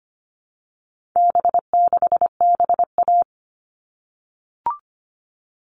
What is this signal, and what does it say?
Morse code, heard as a keyed tone, about 25 wpm, 700 Hz. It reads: B6BA
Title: All Cut Numbers - 4-Digit 25wpm